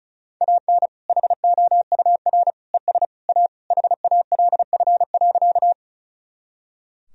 Morse code, heard as a keyed tone, about 36 wpm, 700 Hz. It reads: AN HOUR ES A HALF.
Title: Sentences from Top 400 Words and Abbreviations 35wpm